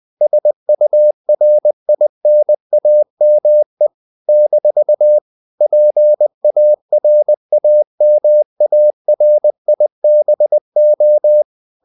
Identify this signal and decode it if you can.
Morse code, heard as a keyed tone, about 20 wpm, 600 Hz. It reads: SURINAME - PARAMARIBO
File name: CW / Practice / NationalCapitals20